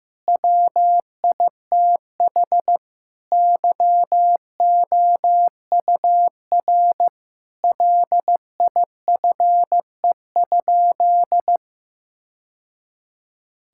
Morse code, heard as a keyed tone, about 15 wpm, 700 Hz. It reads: WITH YOUR LIFE?
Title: Sentences from Top 1000 Words - Review Part 6 - 15wpm